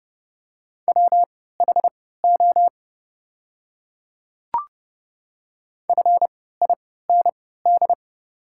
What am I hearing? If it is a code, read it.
Morse code, heard as a keyed tone, about 29 wpm, spaced as for 17 wpm, 690 Hz. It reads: WHO FIND
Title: 3x Character Spacing - Top 100 Words - 30wpm